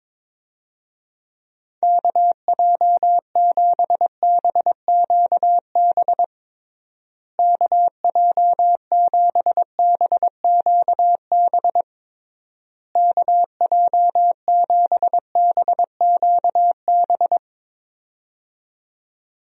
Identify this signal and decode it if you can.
Morse code, heard as a keyed tone, about 22 wpm, 700 Hz. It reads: KJ7BQB KJ7BQB KJ7BQB
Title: USA Call Signs - 6 Characters - Repeated 3x - 22wpm